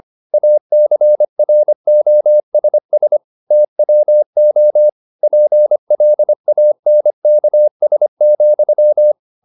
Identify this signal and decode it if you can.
Morse code, heard as a keyed tone, about 25 wpm, 600 Hz. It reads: ACROSS TWO PLANKS,
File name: the old man and the sea part 1 25wpm